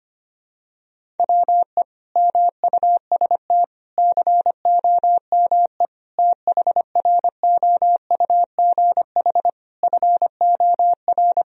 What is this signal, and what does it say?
Morse code, heard as a keyed tone, about 25 wpm, 700 Hz. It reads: WE MUST COME THROUGH FOR
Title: Sentences from Top 1000 Words - Review Part 1 - 25wpm